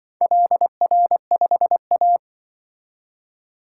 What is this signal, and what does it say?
Morse code, heard as a keyed tone, about 24 wpm, 700 Hz. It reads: LR5A
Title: Speed-Racing - Four Letter-Number Combinations - 20wpm